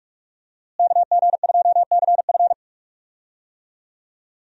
Morse code, heard as a keyed tone, about 46 wpm, 700 Hz. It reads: KG2CF
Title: USA Call Signs - 5 Characters - Repeated 3x - 45wpm